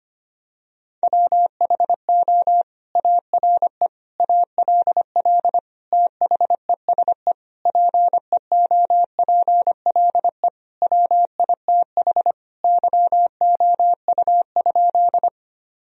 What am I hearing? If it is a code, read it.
Morse code, heard as a keyed tone, about 25 wpm, 700 Hz. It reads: WHO ARE ALL THESE PEOPLE WITH YOU?
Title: Sentences from Top 1000 Words - Review Part 1 - 25wpm